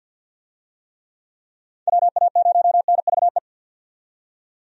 Morse code, heard as a keyed tone, about 52 wpm, 700 Hz. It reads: WA0NVE
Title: USA Call Signs - 6 Characters - 50wpm